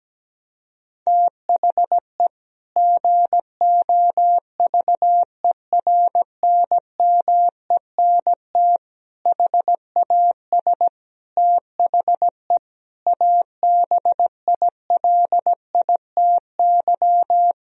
Morse code, heard as a keyed tone, about 17 wpm, 700 Hz. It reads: THE GOVERNMENT HAS THE ABILITY
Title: Sentences from Top 1000 Words - Review Part 3 - 17wpm